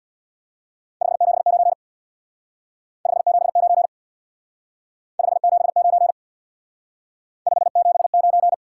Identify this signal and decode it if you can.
Morse code, heard as a keyed tone, about 66 wpm, 700 Hz. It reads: 579 579 579 579